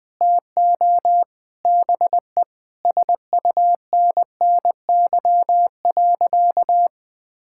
Morse code, heard as a keyed tone, about 20 wpm, 700 Hz. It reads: TO BE SUNNY.
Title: Sentences with 3-Word Phrases - 20wpm